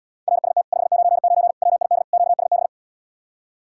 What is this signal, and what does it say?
Morse code, heard as a keyed tone, about 80 wpm, 700 Hz. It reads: RST 599 WID QSB